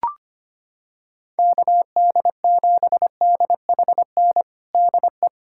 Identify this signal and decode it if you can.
Morse code, heard as a keyed tone, about 25 wpm, 700 Hz. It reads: KD7DHN DE
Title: USA Callsign DE Callsign 25wpm